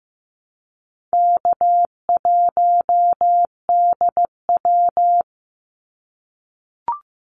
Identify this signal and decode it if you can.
Morse code, heard as a keyed tone, about 15 wpm, 700 Hz. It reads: K1DW
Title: Easy Call Signs - 15wpm